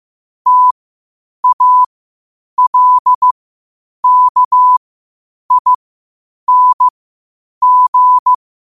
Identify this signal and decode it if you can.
Morse code, heard as a keyed tone, about 15 wpm, spaced as for 8 wpm, 1000 Hz. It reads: TALKING
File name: 8wpm